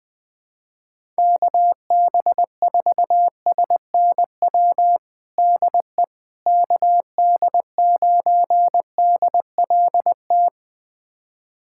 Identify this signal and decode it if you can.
Morse code, heard as a keyed tone, about 20 wpm, 700 Hz. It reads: KB4SNW DE KD9DLT
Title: USA Callsign DE Callsign 20wpm